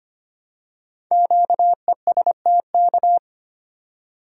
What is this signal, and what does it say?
Morse code, heard as a keyed tone, about 25 wpm, 700 Hz. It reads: QESTK
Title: Code Groups - 5 Letters - 25wpm